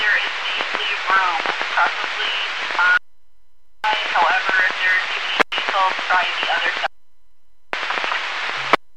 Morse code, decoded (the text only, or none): none